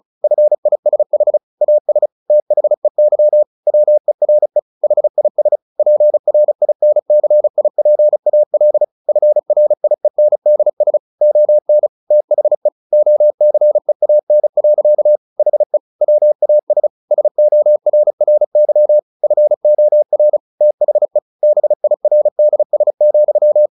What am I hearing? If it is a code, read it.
Morse code, heard as a keyed tone, about 34 wpm, 600 Hz. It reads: FISH AS THEY WERE HIS PRINCIPAL FRIENDS ON THE OCEAN. HE WAS SORRY FOR THE BIRDS,
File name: the old man and the sea / the old man and the sea 1 35wpm